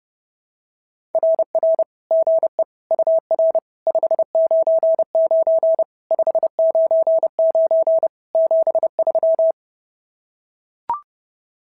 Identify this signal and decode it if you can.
Morse code, heard as a keyed tone, about 26 wpm, 650 Hz. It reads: RR GE UR 599 599 73
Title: SOTA QSO Practice 30wpm